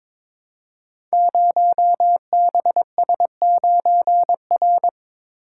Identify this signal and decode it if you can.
Morse code, heard as a keyed tone, about 22 wpm, 700 Hz. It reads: 0BS9R
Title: Code Groups - 5 Letter:Numbers - 22wpm